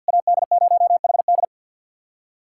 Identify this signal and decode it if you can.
Morse code, heard as a keyed tone, about 50 wpm, 700 Hz. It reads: AD0HD
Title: SOTA QSO Practice 50wpm